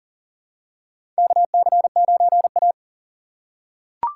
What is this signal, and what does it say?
Morse code, heard as a keyed tone, about 40 wpm, 700 Hz. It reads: KC9A
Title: USA Call Signs - 4 Characters - 6x Character Spacing - 40wpm